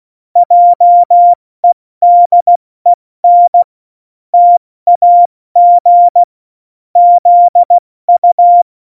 Morse code, heard as a keyed tone, about 16 wpm, 700 Hz. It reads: JEDEN TAG ZU
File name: Morsekurs / Witze / Witz 23. Oktober 2018, 70 F80